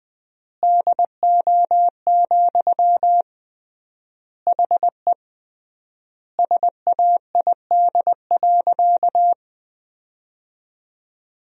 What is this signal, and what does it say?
Morse code, heard as a keyed tone, about 20 wpm, 700 Hz. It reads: DO, HE SAID.
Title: 3x Word-Spacing - Sentences from Top 100 Words - 20wpm